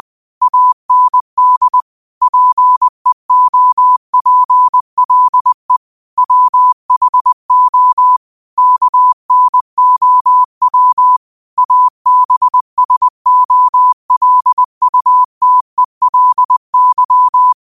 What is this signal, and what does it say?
Morse code, heard as a keyed tone, about 20 wpm, 1000 Hz. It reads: AND PEOPLE WHO KNOW ABSOLUTELY